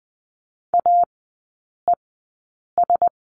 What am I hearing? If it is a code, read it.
Morse code, heard as a keyed tone, about 20 wpm, spaced as for 8 wpm, 700 Hz. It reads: AES
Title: 041 Lesson 8e - Number 1 - Three Characters 20wpm